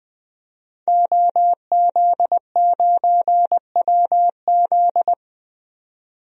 Morse code, heard as a keyed tone, about 20 wpm, 700 Hz. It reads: OZ9WZ
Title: DXCC-100 Call Signs - 5 Characters - 20wpm